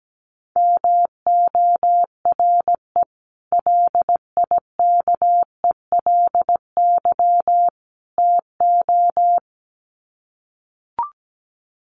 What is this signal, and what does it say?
Morse code, heard as a keyed tone, about 17 wpm, 700 Hz. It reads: MORE LIKELY TO
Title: Sets of 3 Words 17wpm